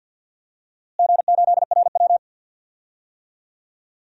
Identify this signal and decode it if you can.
Morse code, heard as a keyed tone, about 53 wpm, 700 Hz. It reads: G8RW
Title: DXCC-100 Call Signs - 4 Characters - Repeated 2x - 50wpm